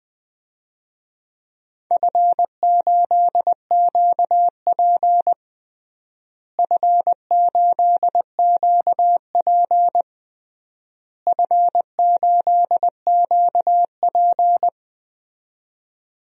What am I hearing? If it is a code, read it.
Morse code, heard as a keyed tone, about 20 wpm, 700 Hz. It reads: F8QP F8QP F8QP
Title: DXCC-100 Call Signs - 4 Characters - Repeated 3x - 20wpm